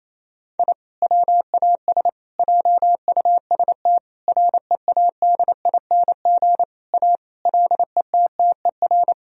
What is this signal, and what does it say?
Morse code, heard as a keyed tone, about 28 wpm, 700 Hz. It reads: I WAS JUST READING A LETTER
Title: Sentences from Top 1000 Words - Review Part 3 - 28wpm